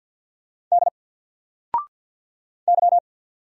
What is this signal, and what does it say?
Morse code, heard as a keyed tone, about 48 wpm, 700 Hz. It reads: D Y
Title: Single Character (Letter:Number:Punctuation:Prosign) 50wpm